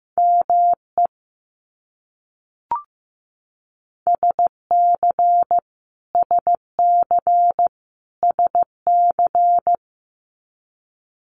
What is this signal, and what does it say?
Morse code, heard as a keyed tone, about 15 wpm, 700 Hz. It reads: ME SC SC SC
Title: US States Repeated 3x 15wpm